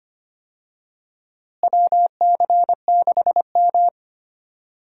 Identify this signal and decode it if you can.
Morse code, heard as a keyed tone, about 25 wpm, 700 Hz. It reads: WC6M
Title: USA Call Signs - 4 Characters - Repeated 2x - 25wpm